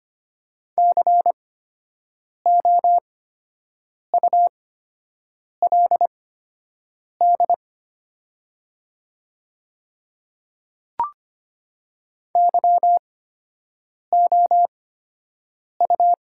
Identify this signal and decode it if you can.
Morse code, heard as a keyed tone, about 25 wpm, spaced as for 7 wpm, 700 Hz. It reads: COULD YOU
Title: 8x Character Spacing - Top 100 Words - 25wpm